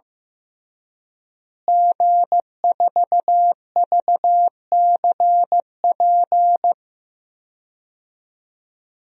Morse code, heard as a keyed tone, about 16 wpm, 700 Hz. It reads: G4VCP
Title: DXCC-100 Call Signs - 5 Characters - Repeated 2x - 15wpm